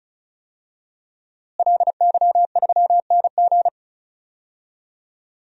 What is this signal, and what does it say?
Morse code, heard as a keyed tone, about 36 wpm, 700 Hz. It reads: LY3NG
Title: DXCC-100 Call Signs - 5 Characters - Repeated 2x - 35wpm